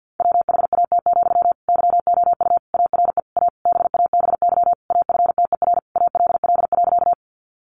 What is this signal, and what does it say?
Morse code, heard as a keyed tone, about 49 wpm, 700 Hz. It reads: WHAT, YOU ARE A BABY AFTER ALL.
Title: Sentences from Top 1300 Words 50wpm